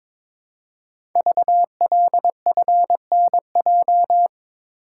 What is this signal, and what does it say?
Morse code, heard as a keyed tone, about 22 wpm, 700 Hz. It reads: VLFNJ
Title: Code Groups - 5 Letters - 22wpm